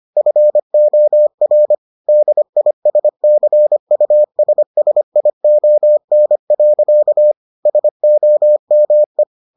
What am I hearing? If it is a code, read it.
Morse code, heard as a keyed tone, about 25 wpm, 600 Hz. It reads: FOR DISCUSSION. SOME